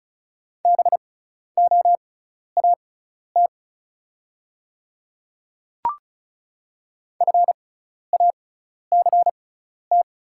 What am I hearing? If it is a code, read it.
Morse code, heard as a keyed tone, about 36 wpm, spaced as for 12 wpm, 700 Hz. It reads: BOAT FACT
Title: 6x Character Spacing - Top 500 Words - 35wpm